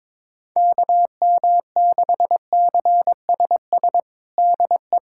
Morse code, heard as a keyed tone, about 22 wpm, 700 Hz. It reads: KM6CSS DE
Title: USA Callsign DE Callsign 22wpm